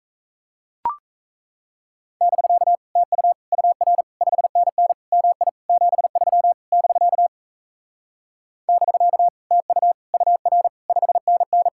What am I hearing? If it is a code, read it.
Morse code, heard as a keyed tone, about 37 wpm, 700 Hz. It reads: <BK> TU UR 5NN MI 73 <BK> <BK> TU UR 5NN